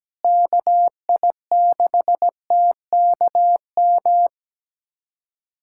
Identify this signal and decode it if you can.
Morse code, heard as a keyed tone, about 17 wpm, 700 Hz. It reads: KI6TKM
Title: USA Callsign DE Callsign - 6 Characters - 17wpm